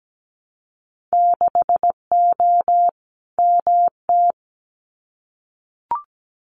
Morse code, heard as a keyed tone, about 17 wpm, 700 Hz. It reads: 6O MT
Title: Winter Field Day Exchanges Encore 17wpm